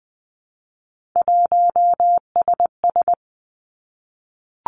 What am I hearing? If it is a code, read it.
Morse code, heard as a keyed tone, about 20 wpm, 700 Hz. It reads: 1SS
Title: Three Challenging Letter-Number Combinations 20wpm